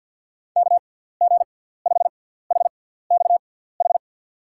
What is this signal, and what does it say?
Morse code, heard as a keyed tone, about 50 wpm, spaced as for 17 wpm, 700 Hz. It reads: KG5HXH
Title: USA Call Signs - 6x Character Spacing - 50wpm